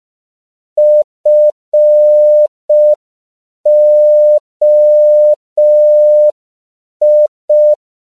Morse code, heard as a keyed tone, about 5 wpm, 600 Hz. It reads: FOI
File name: CW / 5PPM